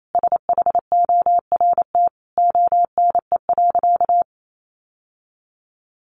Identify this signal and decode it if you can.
Morse code, heard as a keyed tone, about 28 wpm, 700 Hz. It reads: SHORT ONE.
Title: Sentences from Top 1100 Words 28wpm